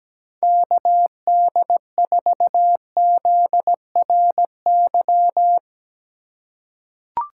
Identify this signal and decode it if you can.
Morse code, heard as a keyed tone, about 17 wpm, 700 Hz. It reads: KD4ZRY
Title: USA Call Signs - 6 Characters - 3x Character Spacing - 17wpm